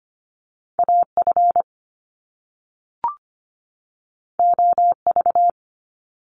Morse code, heard as a keyed tone, about 25 wpm, 700 Hz. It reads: AF OV
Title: Top Two Letter Combinations in Words 25wpm